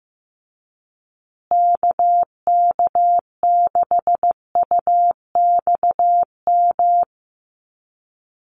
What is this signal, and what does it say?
Morse code, heard as a keyed tone, about 15 wpm, 700 Hz. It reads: KK6UXM
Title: USA Call Signs Repeated 2x 15wpm